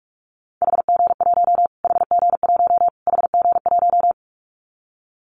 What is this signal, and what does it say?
Morse code, heard as a keyed tone, about 44 wpm, 700 Hz. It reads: HG1 HG1 HG1
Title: Callsign Prefixes for Top 25 DXCC - Repeated 3x - 45wpm